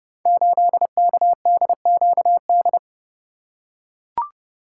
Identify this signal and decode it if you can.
Morse code, heard as a keyed tone, about 30 wpm, 700 Hz. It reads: 8KDQD
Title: Code Groups - 5 Letter:Numbers - 30wpm